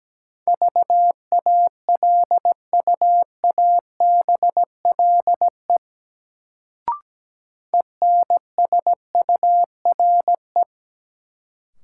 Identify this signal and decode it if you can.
Morse code, heard as a keyed tone, about 17 wpm, 700 Hz. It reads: VALUABLE ENSURE
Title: Sentences from Top 1100 Words - New Word Drill Alt - 17wpm